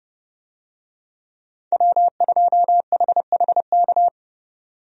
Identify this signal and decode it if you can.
Morse code, heard as a keyed tone, about 31 wpm, 700 Hz. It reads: W2HHK
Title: USA Call Signs - 5 Characters - 30wpm